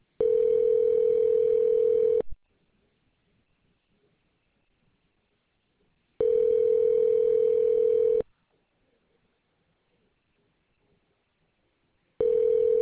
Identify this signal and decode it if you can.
Ringback tone (North America), 440 + 480 Hz.